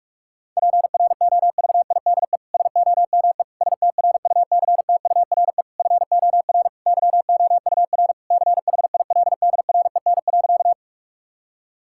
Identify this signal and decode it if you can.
Morse code, heard as a keyed tone, about 46 wpm, 700 Hz. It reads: PROVIDE SOME STRUCTURE FOR YOUR CHILDREN.
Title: Sentences from Top 1000 Words - Review Part 6 - 45wpm